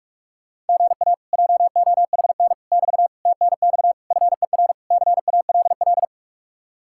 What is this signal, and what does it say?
Morse code, heard as a keyed tone, about 59 wpm, 700 Hz. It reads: GA JOHN <BT> TNX FER CALL